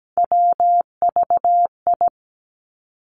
Morse code, heard as a keyed tone, about 17 wpm, 700 Hz. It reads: WVI
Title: Three Challenging Letter-Number Combinations 17wpm